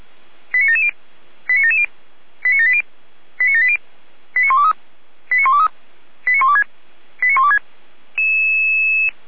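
Five-tone selective call (ZVEI), heard as quick runs of five tones, tones of about 70 ms. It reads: 89770 89770 89780 89780 89123 89123 89126 89126
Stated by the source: ZVEI five-tone selective call received on SDR.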